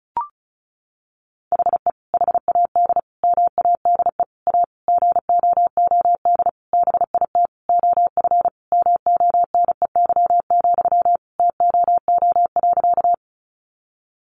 Morse code, heard as a keyed tone, about 34 wpm, 700 Hz. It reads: HE HAD MADE A GOOD BIT OF MONEY, TOO.
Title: Sentences from Top 900 Words 35wpm